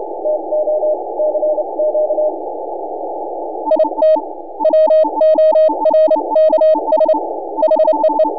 Morse code, heard as a keyed tone, about 27 wpm, 610 Hz, with noise. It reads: TO CQ IT WORKS HEE
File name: n6tr / N6AA Skimmed